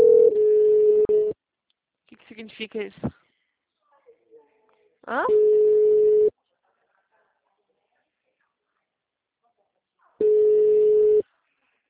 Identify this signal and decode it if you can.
Ringback tone (Europe, 425 Hz), with speech.
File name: out- -2022- - - .722